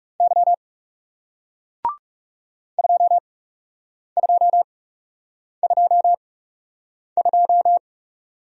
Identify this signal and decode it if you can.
Morse code, heard as a keyed tone, about 40 wpm, spaced as for 8 wpm, 700 Hz. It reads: Y 2222